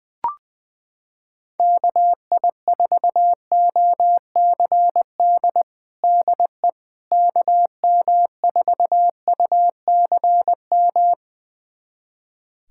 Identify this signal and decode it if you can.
Morse code, heard as a keyed tone, about 21 wpm, 700 Hz. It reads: KI4OCD DE KM4UCM
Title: USA Callsign DE Callsign - 6 Characters - 20wpm